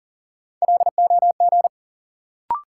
Morse code, heard as a keyed tone, about 40 wpm, 700 Hz. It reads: LOG
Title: Sentences from Top 1400 Words - New Word Drill - 40wpm